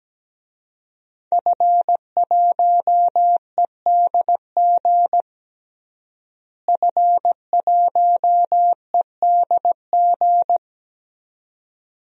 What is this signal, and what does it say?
Morse code, heard as a keyed tone, about 17 wpm, 700 Hz. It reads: F1EDG F1EDG
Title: DXCC-100 Call Signs - 5 Characters - Repeated 2x - 17wpm